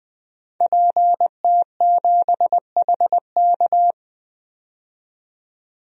Morse code, heard as a keyed tone, about 20 wpm, 700 Hz. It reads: PT7HK
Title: DXCC-100 Call Signs - 5 Characters - 20wpm